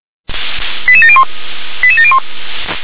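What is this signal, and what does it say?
Five-tone selective call (ZVEI), heard as quick runs of five tones, tones of about 70 ms. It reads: 88701 88701
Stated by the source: ZVEI five-tone selective call received on an SDR.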